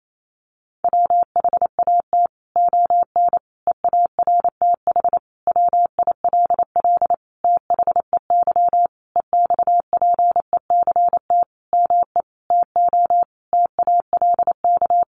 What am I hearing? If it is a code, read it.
Morse code, heard as a keyed tone, about 28 wpm, 700 Hz. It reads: WHAT ON EARTH WILL THEY EXPECT ME TO TALK
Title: Sentences from Top 1100 Words 28wpm